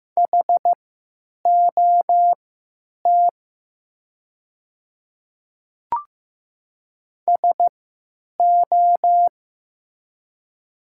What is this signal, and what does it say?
Morse code, heard as a keyed tone, about 15 wpm, spaced as for 9 wpm, 700 Hz. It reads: HOT SO
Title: 3x Character Spacing - Top 100 Words - 15wpm